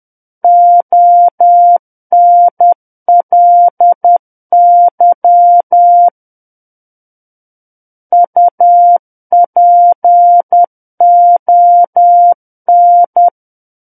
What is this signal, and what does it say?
Morse code, heard as a keyed tone, about 10 wpm, 700 Hz. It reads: ONLY UPON
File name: Book One / 01-Chapter 1 - 10 WPM